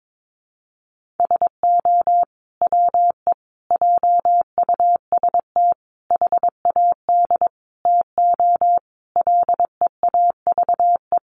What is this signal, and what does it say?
Morse code, heard as a keyed tone, about 22 wpm, 700 Hz. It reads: SO WE JUST HAD TO LEAVE
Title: Sentences from Top 1500 Words 22wpm